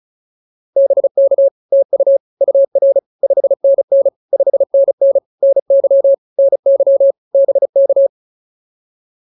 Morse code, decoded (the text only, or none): BK TU UR 5NN 5NN NY NY BK